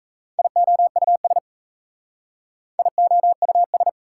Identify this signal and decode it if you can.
Morse code, heard as a keyed tone, about 36 wpm, 690 Hz. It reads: IOUS IOUS